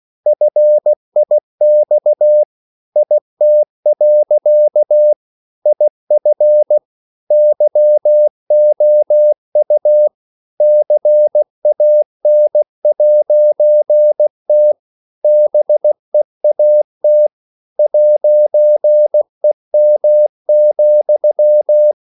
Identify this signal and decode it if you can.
Morse code, heard as a keyed tone, about 16 wpm, 600 Hz. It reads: FIX IT. IF YOU CAN'T BEAT 'EM,
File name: CW / Practice / EnglishSayings16